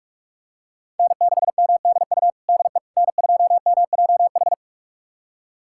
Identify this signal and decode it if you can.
Morse code, heard as a keyed tone, about 42 wpm, 700 Hz. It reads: N6MDU DE N2MJH
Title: USA Callsign DE Callsign - 5 Characters - 45wpm